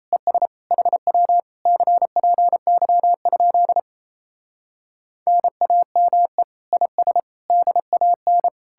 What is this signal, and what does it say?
Morse code, heard as a keyed tone, about 37 wpm, 700 Hz. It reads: ES HW CPY? NAME IS DAN